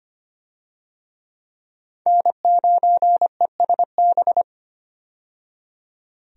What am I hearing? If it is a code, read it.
Morse code, heard as a keyed tone, about 25 wpm, 700 Hz. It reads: N9ESB
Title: USA Call Signs - 5 Characters - 25wpm